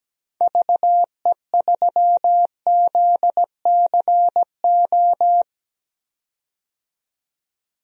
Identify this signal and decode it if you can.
Morse code, heard as a keyed tone, about 17 wpm, 700 Hz. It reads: VE3ZCO
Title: DXCC-100 Call Signs - 6 Characters - 17wpm